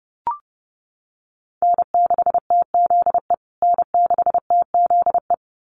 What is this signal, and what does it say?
Morse code, heard as a keyed tone, about 30 wpm, 700 Hz. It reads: N6TZE N6TZE
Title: USA Call Signs Repeated 2x 30wpm